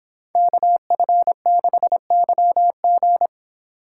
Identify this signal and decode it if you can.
Morse code, heard as a keyed tone, about 27 wpm, 700 Hz. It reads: KF6YG
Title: Speed Racing - USA 2x2 Call Signs 22wpm